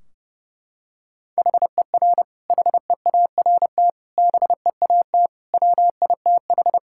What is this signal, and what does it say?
Morse code, heard as a keyed tone, about 30 wpm, 700 Hz. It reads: HER HEART BEAT WITH
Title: Sentences from Top 1600 Words 30wpm